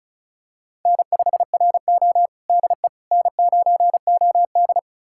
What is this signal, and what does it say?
Morse code, heard as a keyed tone, about 36 wpm, 700 Hz. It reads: N5RO DE N9OD
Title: USA Callsign DE Callsign - 4 Characters - 35wpm